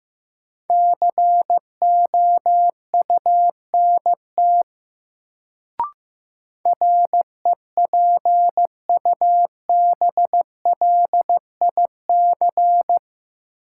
Morse code, heard as a keyed tone, about 15 wpm, 700 Hz. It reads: COUNT REPUBLIC
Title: Sentences from Top 1500 Words - New Word Drill Alt - 15wpm